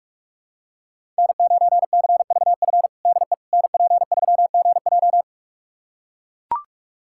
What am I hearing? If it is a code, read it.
Morse code, heard as a keyed tone, about 42 wpm, 700 Hz. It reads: N9CVF DE NP3GJ
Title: USA Callsign DE Callsign - 5 Characters - 45wpm